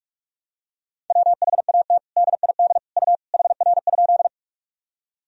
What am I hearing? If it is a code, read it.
Morse code, heard as a keyed tone, about 41 wpm, 700 Hz. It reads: WHAT DID U HR?
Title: Sentences from Top 300 Words and Abbreviations 45wpm